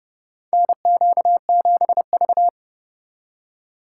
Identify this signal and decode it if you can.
Morse code, heard as a keyed tone, about 30 wpm, 700 Hz. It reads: NQ7V
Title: USA Call Signs - 4 Characters - 6x Character Spacing - 30wpm